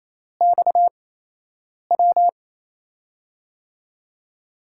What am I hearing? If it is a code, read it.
Morse code, heard as a keyed tone, about 28 wpm, 700 Hz. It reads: X W
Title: USA Call Signs - 6 Characters - 8x Character Spacing - 28wpm